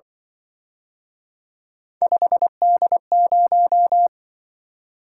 Morse code, heard as a keyed tone, about 24 wpm, 700 Hz. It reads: 5D0
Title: Speed-Racing - Three Letter-Number Combinations - 20wpm